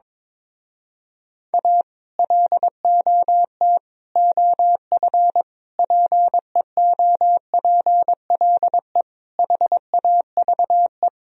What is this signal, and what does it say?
Morse code, heard as a keyed tone, about 22 wpm, 700 Hz. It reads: A LOT OF PEOPLE HAVE
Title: Speed-Racing - Sets of 5 Words 22wpm